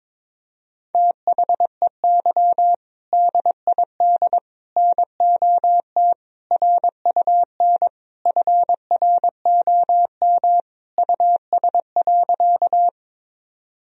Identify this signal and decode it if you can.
Morse code, heard as a keyed tone, about 21 wpm, 700 Hz. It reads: THEY DID NOT RUN FROM US.
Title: Sentences from Top 1000 Words - Review Part 1 - 22wpm